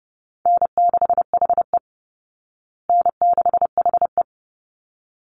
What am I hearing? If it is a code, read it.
Morse code, heard as a keyed tone, about 30 wpm, 700 Hz. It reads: N6HE N6HE
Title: Call Signs Repeated 3x - 30wpm